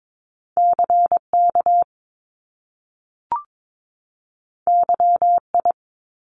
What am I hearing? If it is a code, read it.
Morse code, heard as a keyed tone, about 22 wpm, 700 Hz. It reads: CK YI
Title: Top Two Letter Combinations in Words 22wpm